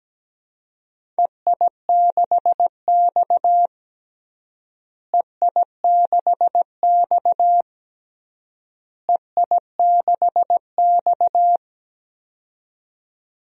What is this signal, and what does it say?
Morse code, heard as a keyed tone, about 17 wpm, 700 Hz. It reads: EI6X EI6X EI6X
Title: DXCC-100 Call Signs - 4 Characters - Repeated 3x - 17wpm